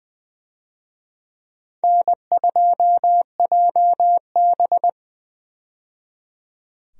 Morse code, heard as a keyed tone, about 20 wpm, 700 Hz. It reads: N2JB
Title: USA Call Signs - 4 Characters - 20wpm